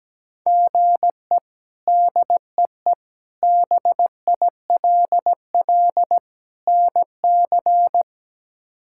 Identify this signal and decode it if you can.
Morse code, heard as a keyed tone, about 17 wpm, 700 Hz. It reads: GE DEE BILL NC
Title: K1USN SST QSO Practice - 17wpm